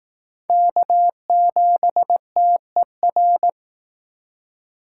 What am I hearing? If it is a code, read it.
Morse code, heard as a keyed tone, about 19 wpm, 700 Hz. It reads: K7TER